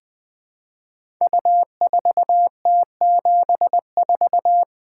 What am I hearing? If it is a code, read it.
Morse code, heard as a keyed tone, about 20 wpm, 700 Hz. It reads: U4T74
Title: Code Groups - 5 Letter:Numbers - 20wpm